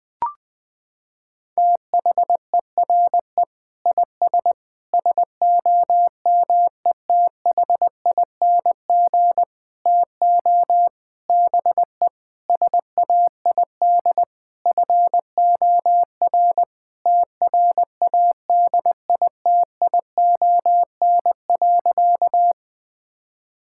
Morse code, heard as a keyed tone, about 21 wpm, 700 Hz. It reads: THERE IS SOMETHING TO BE SAID FOR TRADITION.
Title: Sentences with 7-Word Phrases - 20wpm